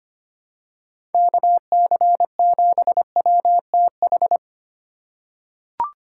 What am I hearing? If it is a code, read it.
Morse code, heard as a keyed tone, about 25 wpm, 700 Hz. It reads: KC7WTH
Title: USA Call Signs - 3x Character Spacing - 25wpm